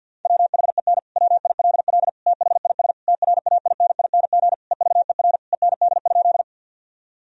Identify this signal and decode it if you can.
Morse code, heard as a keyed tone, about 50 wpm, 700 Hz. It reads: WHEN WILL THIS TRAINING EVER END?
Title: Sentences from Top 1000 Words - Review Part 4 - 50wpm